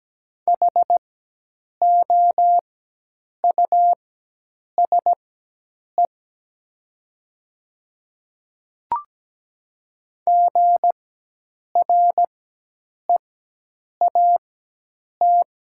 Morse code, heard as a keyed tone, about 17 wpm, spaced as for 8 wpm, 700 Hz. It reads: HOUSE GREAT
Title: 4x Character Spacing - Top 200 Words - 17wpm